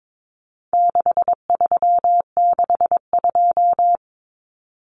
Morse code, heard as a keyed tone, about 22 wpm, 700 Hz. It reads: 6362
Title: Numbers - 4-Digit 22wpm